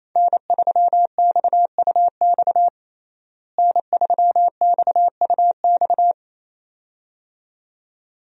Morse code, heard as a keyed tone, about 28 wpm, 700 Hz. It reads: N3XUX N3XUX
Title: USA Call Signs - 5 Characters - Repeated 2x - 28wpm